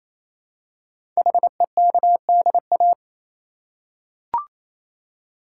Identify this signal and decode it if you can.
Morse code, heard as a keyed tone, about 27 wpm, 700 Hz. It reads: HEKDA
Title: Code Groups - 5 Letters - 28wpm